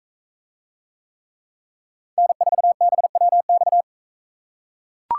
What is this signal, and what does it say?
Morse code, heard as a keyed tone, about 43 wpm, 700 Hz. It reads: N4BWX